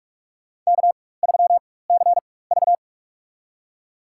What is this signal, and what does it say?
Morse code, heard as a keyed tone, about 42 wpm, spaced as for 20 wpm, 700 Hz. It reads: K3CV
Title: USA Call Signs - 4 Characters - 4x Character Spacing - 45wpm